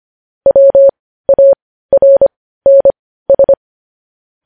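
Morse code, heard as a keyed tone, about 25 wpm, spaced as for 15 wpm, 550 Hz. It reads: WARNS
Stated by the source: Morse code transcription of news headlines.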